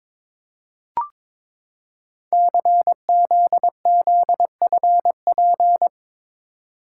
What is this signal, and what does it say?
Morse code, heard as a keyed tone, about 22 wpm, 700 Hz. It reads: CZZFP
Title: Code Groups - 5 Letter:Numbers - 22wpm